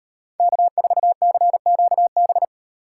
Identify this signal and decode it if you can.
Morse code, heard as a keyed tone, about 38 wpm, 700 Hz. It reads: K4CQB